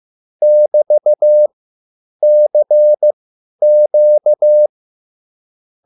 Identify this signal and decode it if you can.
Morse code, heard as a keyed tone, about 15 wpm, spaced as for 9 wpm, 600 Hz. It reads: <BT>CQ
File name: QSL / CW lesson